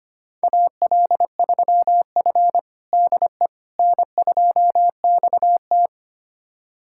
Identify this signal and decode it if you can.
Morse code, heard as a keyed tone, about 25 wpm, 700 Hz. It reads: AL3F DE N2XT
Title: USA Callsign DE Callsign - 4 Characters - 25wpm